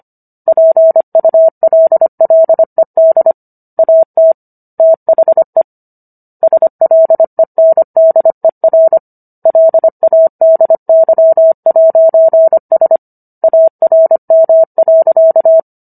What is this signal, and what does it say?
Morse code, heard as a keyed tone, about 25 wpm, 650 Hz. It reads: PULLED AT THE SLENDER LADY'S ARM.